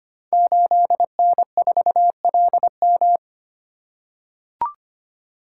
Morse code, heard as a keyed tone, about 25 wpm, 700 Hz. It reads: 8N4LM
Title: Code Groups - 5 Letter:Numbers - 25wpm